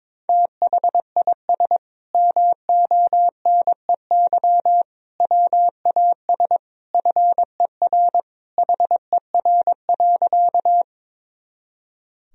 Morse code, heard as a keyed tone, about 22 wpm, 700 Hz. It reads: THIS MONEY WAS FER HER.
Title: Sentences from Top 400 Words and Abbreviations 22wpm